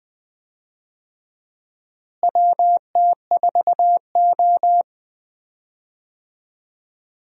Morse code, heard as a keyed tone, about 20 wpm, 700 Hz. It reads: WT4O
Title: USA Call Signs - 4 Characters - Repeated 2x - 20wpm